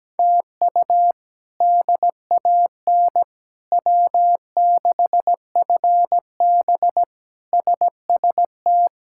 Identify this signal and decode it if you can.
Morse code, heard as a keyed tone, about 17 wpm, 700 Hz. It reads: TU DAN W6FB SST
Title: K1USN SST QSO Practice - 17wpm